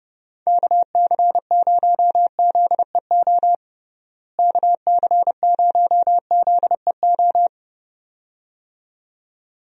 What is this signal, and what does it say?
Morse code, heard as a keyed tone, about 30 wpm, 700 Hz. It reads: KC0ZEO KC0ZEO
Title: USA Call Signs - 6 Characters - Repeated 2x - 30wpm